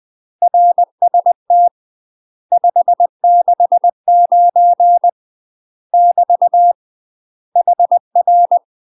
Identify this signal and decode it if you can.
Morse code, heard as a keyed tone, about 20 wpm, 700 Hz. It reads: RST 569 <BT> HR
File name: UK MASTER QSOs TRAINING / 20WPM / UK MASTER QSOs 0046